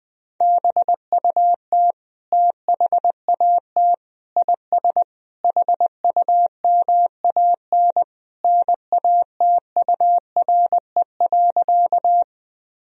Morse code, heard as a keyed tone, about 20 wpm, 700 Hz. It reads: BUT THAT IS HUMAN NATURE.
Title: Sentences from Top 1000 Words - Review Part 3 - 20wpm